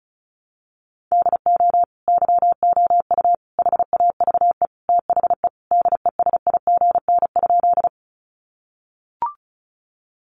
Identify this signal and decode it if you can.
Morse code, heard as a keyed tone, about 34 wpm, 700 Hz. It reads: DO YOU HAVE THE DESIGN?
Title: Sentences from Top 800 Words 35wpm